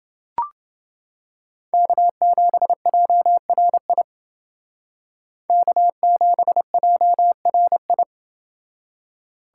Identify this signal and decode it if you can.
Morse code, heard as a keyed tone, about 28 wpm, 700 Hz. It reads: K7JRI K7JRI